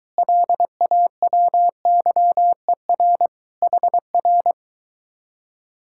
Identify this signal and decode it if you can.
Morse code, heard as a keyed tone, about 22 wpm, 700 Hz. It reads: LAWYER HR